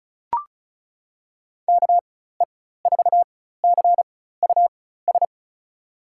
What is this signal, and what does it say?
Morse code, heard as a keyed tone, about 36 wpm, spaced as for 16 wpm, 700 Hz. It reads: KE4CUS
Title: USA Call Signs - 6 Characters - 4x Character Spacing - 35wpm